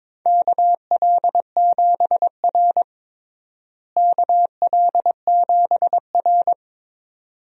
Morse code, heard as a keyed tone, about 22 wpm, 700 Hz. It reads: KL7R KL7R
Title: DXCC-100 Call Signs - 4 Characters - Repeated 2x - 22wpm